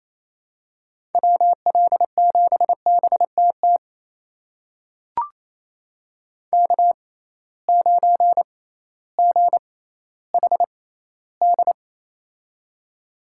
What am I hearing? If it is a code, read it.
Morse code, heard as a keyed tone, about 28 wpm, spaced as for 15 wpm, 700 Hz. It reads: WL7BTT K9GHD
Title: USA Call Signs - 6x Character Spacing - 28wpm